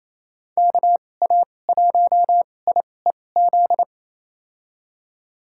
Morse code, heard as a keyed tone, about 28 wpm, spaced as for 20 wpm, 700 Hz. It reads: KA1IEZ
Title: USA Call Signs - 6 Characters - 2x Character Spacing - 28wpm